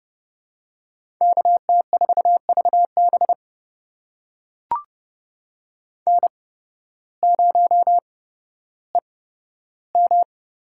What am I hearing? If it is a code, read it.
Morse code, heard as a keyed tone, about 32 wpm, spaced as for 14 wpm, 700 Hz. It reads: KT4VB N0EM
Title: USA Call Signs - 5 Characters - 8x Character Spacing - 30wpm